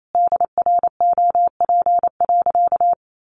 Morse code, heard as a keyed tone, about 28 wpm, 700 Hz. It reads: DROP.
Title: Sentences from Top 1400 Words 28wpm